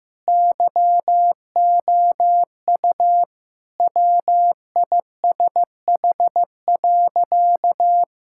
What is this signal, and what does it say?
Morse code, heard as a keyed tone, about 16 wpm, 700 Hz. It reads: YOU WISH.
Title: Sentences from Top 1600 Words 15wpm